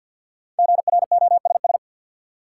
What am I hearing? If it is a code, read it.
Morse code, heard as a keyed tone, about 50 wpm, 700 Hz. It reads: GROSS
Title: Five Letter Words - 50wpm